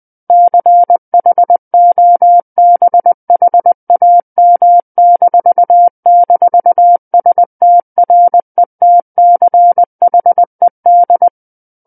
Morse code, heard as a keyed tone, about 20 wpm, 700 Hz. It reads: CHOBHAM--STRETCHED
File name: 20 WPM